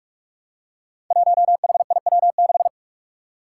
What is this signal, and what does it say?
Morse code, heard as a keyed tone, about 46 wpm, 700 Hz. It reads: 1HIW6
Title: Five Challenging Letter-Number Combinations 45wpm